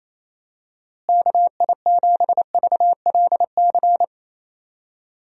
Morse code, heard as a keyed tone, about 28 wpm, 700 Hz. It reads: KI7VLC
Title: USA Call Signs - 6 Characters - Repeated 3x - 28wpm